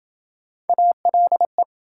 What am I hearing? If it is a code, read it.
Morse code, heard as a keyed tone, about 27 wpm, 700 Hz. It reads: ALE